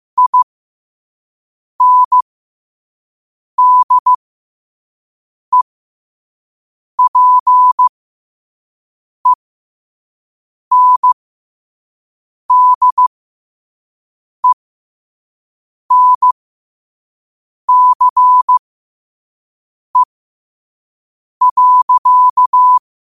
Morse code, heard as a keyed tone, about 15 wpm, spaced as for 5 wpm, 1000 Hz. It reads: INDEPENDENCE.
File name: qotdfiles / 5wpm